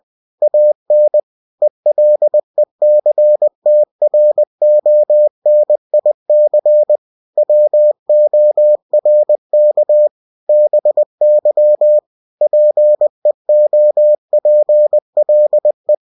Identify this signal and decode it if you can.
Morse code, heard as a keyed tone, about 20 wpm, 600 Hz. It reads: AN ELECTRONIC WORK BY PEOPLE
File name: cw-books / winnie-the-pooh / winnie-20wpm-page-0113